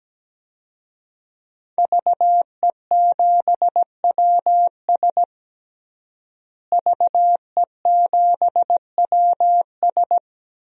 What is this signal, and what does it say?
Morse code, heard as a keyed tone, about 17 wpm, 700 Hz. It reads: VE7WS VE7WS
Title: DXCC-100 Call Signs - 5 Characters - Repeated 2x - 17wpm